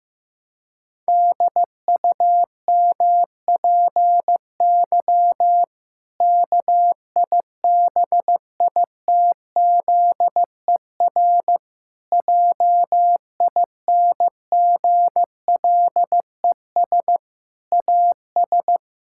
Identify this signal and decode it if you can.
Morse code, heard as a keyed tone, about 15 wpm, 700 Hz. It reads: DUMPY KIBITZER JINGLES AS
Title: Pangram Challenge v2 - 15wpm